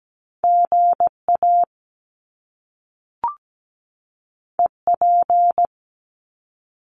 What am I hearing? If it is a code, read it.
Morse code, heard as a keyed tone, about 17 wpm, 700 Hz. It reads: GA EP
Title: Top Two Letter Combinations in Words 17wpm